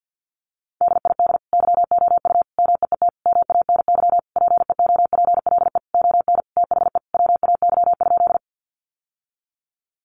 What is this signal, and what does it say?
Morse code, heard as a keyed tone, about 50 wpm, 700 Hz. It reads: DID YOU MEET MANY PEOPLE ON THE WAY?
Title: Sentences from Top 1100 Words 50wpm